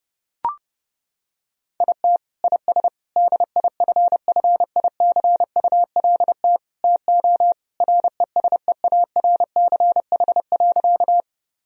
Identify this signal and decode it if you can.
Morse code, heard as a keyed tone, about 30 wpm, 700 Hz. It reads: IT IS DIFFICULT TO RESEARCH.
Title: Sentences from Top 1000 Words - Review Part 4 - 30wpm